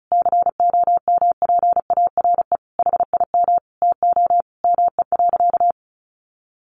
Morse code, heard as a keyed tone, about 34 wpm, 700 Hz. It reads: COMPARE HIM TO ME.
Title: Sentences from Top 1300 Words 35wpm